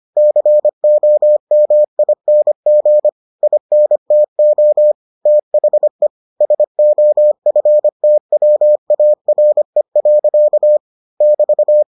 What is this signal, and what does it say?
Morse code, heard as a keyed tone, about 25 wpm, 600 Hz. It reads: COMING INTO THE SOFTWARE. <BT>